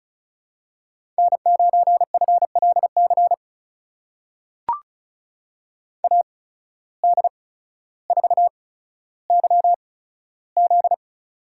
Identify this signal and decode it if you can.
Morse code, heard as a keyed tone, about 34 wpm, spaced as for 15 wpm, 700 Hz. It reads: N9FLC AD4YZ
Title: USA Call Signs - 5 Characters - 8x Character Spacing - 35wpm